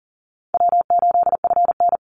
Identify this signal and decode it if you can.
Morse code, heard as a keyed tone, about 40 wpm, 700 Hz. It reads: W8FN
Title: Easy Call Signs - 40wpm